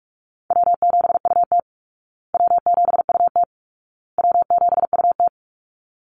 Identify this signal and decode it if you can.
Morse code, heard as a keyed tone, about 44 wpm, 700 Hz. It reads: W7UT W7UT W7UT
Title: Call Signs Repeated 2x - 45wpm